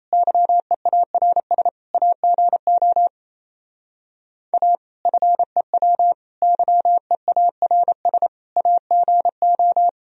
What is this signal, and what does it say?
Morse code, heard as a keyed tone, about 35 wpm, 700 Hz. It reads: YEARS AGO A FEW YEARS AGO